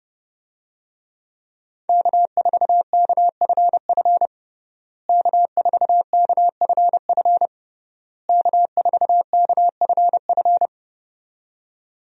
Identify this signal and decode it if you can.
Morse code, heard as a keyed tone, about 30 wpm, 700 Hz. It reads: K4KFF K4KFF K4KFF
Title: USA Call Signs - 5 Characters - Repeated 3x - 30wpm